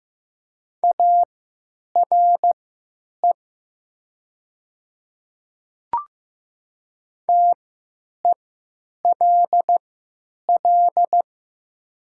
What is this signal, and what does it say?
Morse code, heard as a keyed tone, about 15 wpm, spaced as for 9 wpm, 700 Hz. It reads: ARE TELL
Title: 3x Character Spacing - Top 300 Words - 15wpm